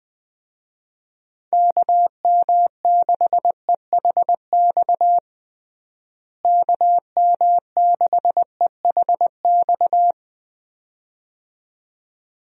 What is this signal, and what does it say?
Morse code, heard as a keyed tone, about 20 wpm, 700 Hz. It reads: KM6EHX KM6EHX
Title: USA Call Signs - 6 Characters - Repeated 2x - 20wpm